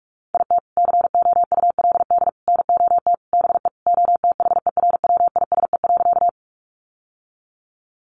Morse code, heard as a keyed tone, about 44 wpm, 700 Hz. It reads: IT COULD NOT BE OTHERWISE.
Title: Sentences from Top 1200 Words 45wpm